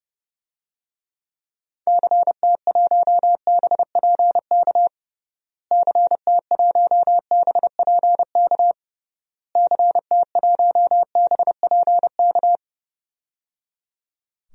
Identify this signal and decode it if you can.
Morse code, heard as a keyed tone, about 30 wpm, 700 Hz. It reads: CT1BPK CT1BPK CT1BPK
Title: DXCC-100 Call Signs - 6 Characters - Repeated 3x - 30wpm